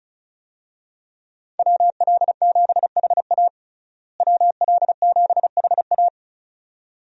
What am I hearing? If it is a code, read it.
Morse code, heard as a keyed tone, about 36 wpm, 700 Hz. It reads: WL7HA WL7HA
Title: USA Call Signs - 5 Characters - Repeated 2x - 35wpm